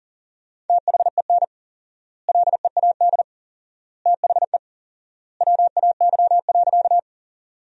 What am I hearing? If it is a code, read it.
Morse code, heard as a keyed tone, about 37 wpm, 700 Hz. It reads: THEN LEAD THE WAY.
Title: 4x Word-Spacing - Sentences from Top 500 Words - 40wpm